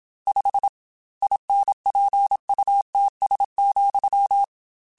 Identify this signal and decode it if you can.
Morse code, heard as a keyed tone, about 27 wpm, 800 Hz. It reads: 5 INPUTS,
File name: Technical Subjects / Technical Subjects-mp3s / AND Gate @25